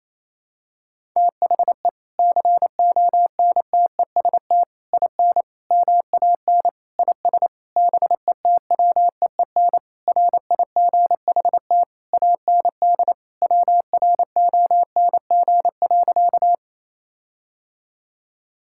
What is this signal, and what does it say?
Morse code, heard as a keyed tone, about 28 wpm, 700 Hz. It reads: THE CONTEST IN MAN IS BETWEEN RIGHT AND WRONG.
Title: Sentences from Top 1600 Words 28wpm